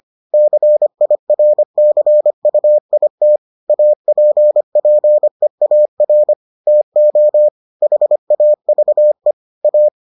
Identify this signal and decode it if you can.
Morse code, heard as a keyed tone, about 25 wpm, 600 Hz. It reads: CIRCUIT APPEAR TO HAVE A